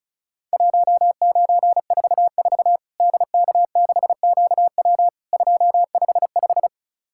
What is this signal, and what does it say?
Morse code, heard as a keyed tone, about 36 wpm, 700 Hz. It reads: 1944 DK6QW 255
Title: Worked All Europe DX Contest - QTC Exchange - 35wpm